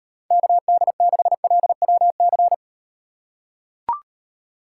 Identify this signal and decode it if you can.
Morse code, heard as a keyed tone, about 38 wpm, 700 Hz. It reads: KD6LWC